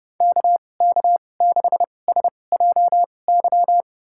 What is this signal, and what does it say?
Morse code, heard as a keyed tone, about 30 wpm, spaced as for 22 wpm, 700 Hz. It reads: KK6SJY
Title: USA Call Signs - 6 Characters - 2x Character Spacing - 30wpm